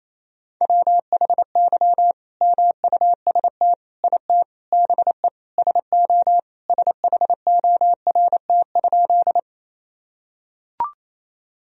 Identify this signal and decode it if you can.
Morse code, heard as a keyed tone, about 28 wpm, 700 Hz. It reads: WHY MUST IT BE SO SHORT?
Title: Sentences from Top 1000 Words - Review Part 2 - 28wpm